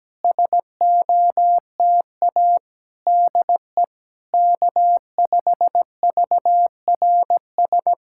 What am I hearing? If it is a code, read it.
Morse code, heard as a keyed tone, about 17 wpm, 700 Hz. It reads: SOTA DE K5VRS
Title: SOTA QSO Practice 17wpm